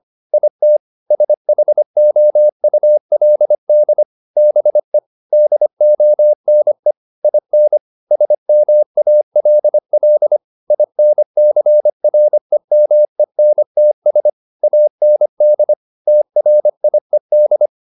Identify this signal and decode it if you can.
Morse code, heard as a keyed tone, about 25 wpm, 600 Hz. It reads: IT SHOULD BE DONE IN SMALL INCREMENTS AND TRIED